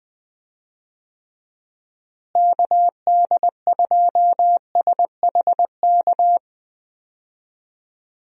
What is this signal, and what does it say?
Morse code, heard as a keyed tone, about 20 wpm, 700 Hz. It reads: KD2SHK
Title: USA Call Signs - 6 Characters - Repeated 3x - 20wpm